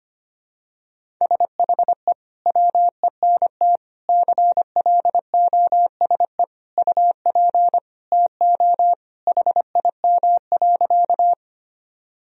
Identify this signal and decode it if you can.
Morse code, heard as a keyed tone, about 25 wpm, 700 Hz. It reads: SHE WENT CLOSE UP TO HIM.
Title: Sentences from Top 1000 Words - Review Part 1 - 25wpm